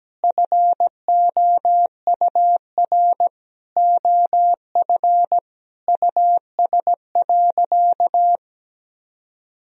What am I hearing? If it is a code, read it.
Morse code, heard as a keyed tone, about 17 wpm, 700 Hz. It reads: FOUR OF US.
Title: Sentences from Top 300 Words and Abbreviations 17wpm